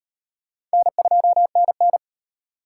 Morse code, heard as a keyed tone, about 37 wpm, 700 Hz. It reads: N2NN